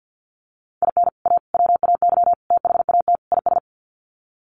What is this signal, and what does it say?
Morse code, heard as a keyed tone, about 50 wpm, 700 Hz. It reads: IN A WAY THAT IS
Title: Sets of 5 Words Encore 50wpm